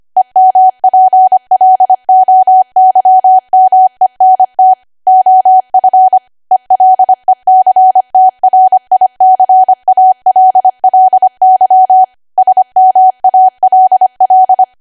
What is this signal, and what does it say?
Morse code, heard as a keyed tone, about 25 wpm, 750 Hz. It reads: EMPLOYMENT OF ELECTRICALLY SMALL
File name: Archive / 25 WPM / 25WPM